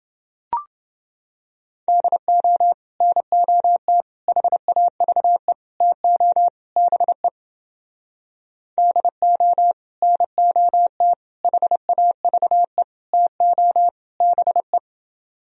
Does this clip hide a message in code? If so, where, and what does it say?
Yes: Morse code from 1.9 to 14.8 s, heard as a keyed tone, about 29 wpm, 700 Hz. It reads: DO NOT HAVE TO BE DO NOT HAVE TO BE